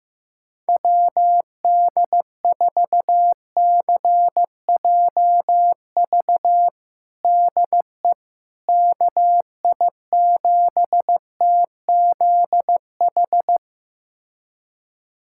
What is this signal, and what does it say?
Morse code, heard as a keyed tone, about 15 wpm, 700 Hz. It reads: WD4CJV DE KI7TZH
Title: USA Callsign DE Callsign - 6 Characters - 15wpm